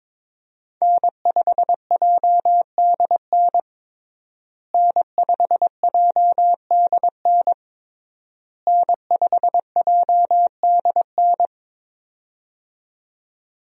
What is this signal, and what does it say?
Morse code, heard as a keyed tone, about 22 wpm, 700 Hz. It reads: N5JDN N5JDN N5JDN
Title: USA Call Signs - 5 Characters - Repeated 3x - 22wpm